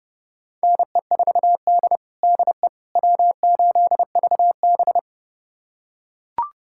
Morse code, heard as a keyed tone, about 32 wpm, 700 Hz. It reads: NE4D DE W8VB
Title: USA Callsign DE Callsign - 4 Characters - 30wpm